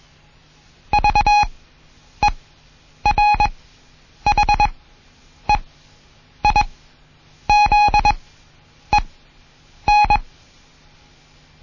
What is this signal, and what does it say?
Morse code, heard as a keyed tone, about 21 wpm, spaced as for 9 wpm, 830 Hz. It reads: VERHEIZEN
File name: Morsen / Kurs A 20